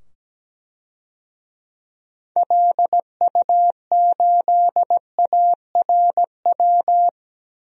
Morse code, heard as a keyed tone, about 17 wpm, 700 Hz. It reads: LU8ARW
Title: DXCC-100 Call Signs - 6 Characters - 17wpm